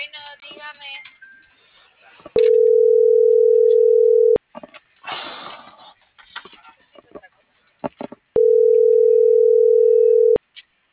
Ringback tone (North America), 440 + 480 Hz.